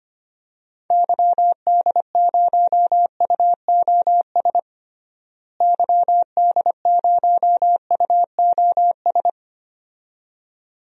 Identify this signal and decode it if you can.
Morse code, heard as a keyed tone, about 25 wpm, 700 Hz. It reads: YD0UOS YD0UOS
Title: DXCC-100 Call Signs - 6 Characters - Repeated 2x - 25wpm